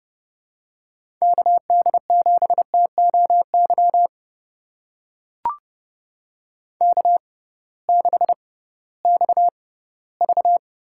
Morse code, heard as a keyed tone, about 30 wpm, spaced as for 17 wpm, 700 Hz. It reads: KD7TOY K6XV
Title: USA Call Signs - 6x Character Spacing - 30wpm